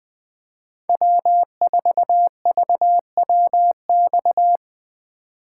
Morse code, heard as a keyed tone, about 20 wpm, 700 Hz. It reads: W4VWX
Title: Speed Racing - USA 1x3 Call Signs 17wpm